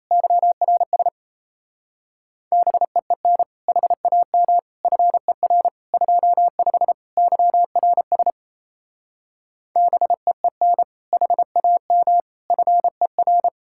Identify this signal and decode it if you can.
Morse code, heard as a keyed tone, about 36 wpm, 700 Hz. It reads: YRS BEEN HAM FER 25 YRS BEEN HAM FER